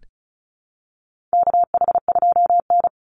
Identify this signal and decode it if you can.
Morse code, heard as a keyed tone, about 34 wpm, 700 Hz. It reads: KH2N
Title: Call Signs Repeated 3x - 35wpm